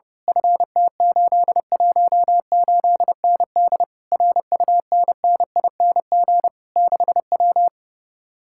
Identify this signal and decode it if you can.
Morse code, heard as a keyed tone, about 28 wpm, 700 Hz. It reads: FT818ND RUNNING 6W
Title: Speed-Racing Common QSO Phrases - 30wpm